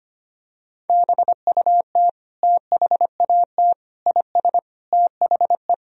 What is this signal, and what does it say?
Morse code, heard as a keyed tone, about 25 wpm, 700 Hz. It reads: BUT THAT IS THE
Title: Sentences from Top 1000 Words - Review Part 3 - 25wpm